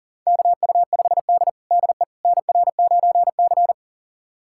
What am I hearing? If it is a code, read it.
Morse code, heard as a keyed tone, about 40 wpm, 700 Hz. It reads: KU5D DE NR9C
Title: USA Callsign DE Callsign - 4 Characters - 40wpm